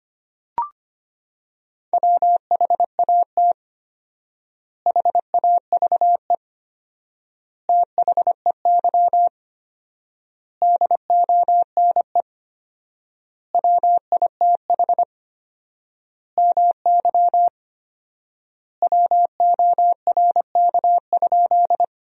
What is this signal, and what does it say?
Morse code, heard as a keyed tone, about 25 wpm, 700 Hz. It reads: WHAT HAVE THEY DONE WITH MY WORK?
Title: 4x Word-Spacing - Sentences from Top 500 Words - 25wpm